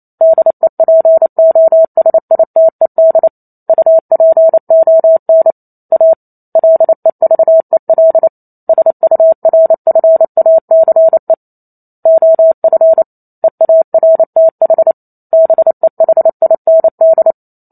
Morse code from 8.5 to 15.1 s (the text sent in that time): SURFACE OF EARTH